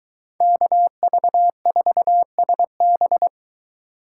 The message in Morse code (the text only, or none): KV4SB